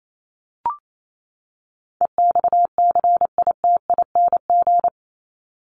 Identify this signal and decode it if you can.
Morse code, heard as a keyed tone, about 28 wpm, 700 Hz. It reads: EXCITING
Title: Top 1751-2000 Words 28wpm